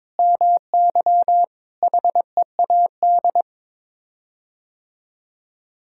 Morse code, heard as a keyed tone, about 22 wpm, 700 Hz. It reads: MY HEAD
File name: practice / 22wpm / Sets of 6 Words - 22wpm